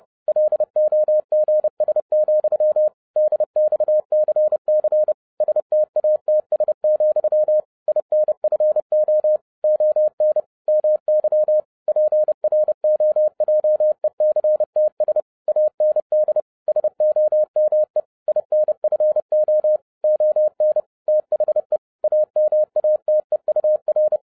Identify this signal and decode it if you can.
Morse code, heard as a keyed tone, about 30 wpm, 610 Hz. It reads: LOGS, DXCC STATS, INFO ON MY PROJECTS AND SOME INFO ON THE AMATEUR